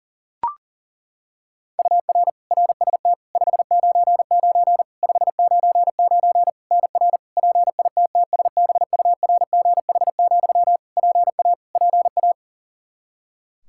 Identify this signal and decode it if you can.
Morse code, heard as a keyed tone, about 37 wpm, 700 Hz. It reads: UR RST 599 599 NR PITTSBURGH, PA PA
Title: Practice QSO - Exchange 1 - Beginning - 40wpm